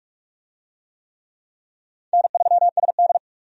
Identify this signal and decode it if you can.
Morse code, heard as a keyed tone, about 41 wpm, 700 Hz. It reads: N3SD
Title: USA Call Signs - 4 Characters - 45wpm